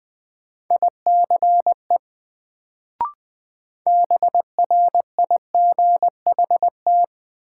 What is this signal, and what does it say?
Morse code, heard as a keyed tone, about 20 wpm, 700 Hz. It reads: ICE BRIGHT
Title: Sentences from Top 1000 Words - New Word Drill - 20wpm